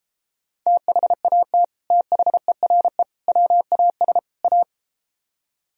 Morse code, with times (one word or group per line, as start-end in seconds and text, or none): 0.7-1.6 THAT
1.9-3.0 THERE
3.3-4.2 WAS
4.4-4.6 A